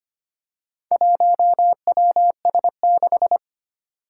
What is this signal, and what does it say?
Morse code, heard as a keyed tone, about 25 wpm, 700 Hz. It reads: 1WS6
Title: Four Challenging Letter-Number Combinations 25wpm